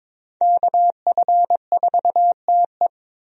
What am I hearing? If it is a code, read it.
Morse code, heard as a keyed tone, about 22 wpm, 700 Hz. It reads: KF4TE
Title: USA Call Signs - 5 Characters - 22wpm